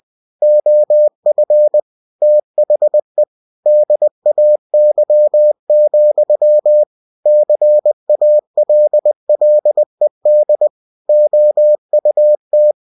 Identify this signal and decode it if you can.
Morse code, heard as a keyed tone, about 20 wpm, 600 Hz. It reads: OF THE DAY, CALLED OUT